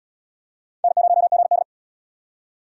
Morse code, heard as a keyed tone, about 78 wpm, 700 Hz. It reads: N0GD